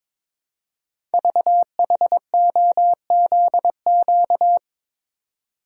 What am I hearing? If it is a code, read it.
Morse code, heard as a keyed tone, about 22 wpm, 700 Hz. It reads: VHOZQ
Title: Code Groups - 5 Letters - 22wpm